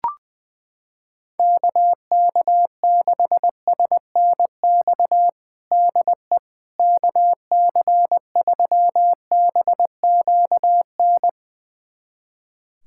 Morse code, heard as a keyed tone, about 20 wpm, 700 Hz. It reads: KK6SNX DE KC3BQN
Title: USA Callsign DE Callsign - 6 Characters - 20wpm